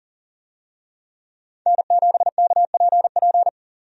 Morse code, heard as a keyed tone, about 40 wpm, 700 Hz. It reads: N7KPP
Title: USA Call Signs - 5 Characters - 40wpm